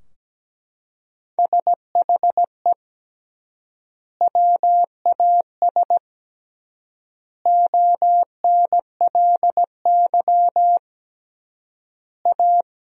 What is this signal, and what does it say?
Morse code, heard as a keyed tone, about 17 wpm, 700 Hz. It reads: SHE WAS ONLY A
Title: 3x Word-Spacing - Sentences from Top 400 Words - 17wpm